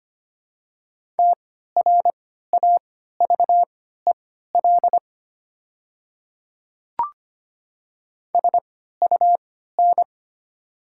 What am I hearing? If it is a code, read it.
Morse code, heard as a keyed tone, about 25 wpm, spaced as for 14 wpm, 700 Hz. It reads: TRAVEL SUN
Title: 3x Character Spacing - Top 500 Words - 25wpm